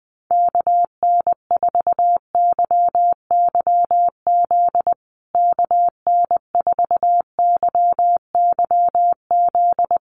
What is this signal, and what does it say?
Morse code, heard as a keyed tone, about 20 wpm, 700 Hz. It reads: KN4YYZ KN4YYZ
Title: USA Call Signs Repeated 3x 20wpm